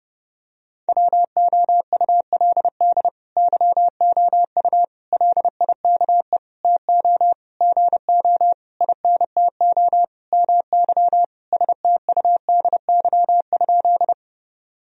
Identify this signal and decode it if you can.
Morse code, heard as a keyed tone, about 32 wpm, 700 Hz. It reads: WOULD YOU LIKE TO GO INTO MY STUDY?
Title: Sentences from Top 1000 Words - Review Part 1 - 30wpm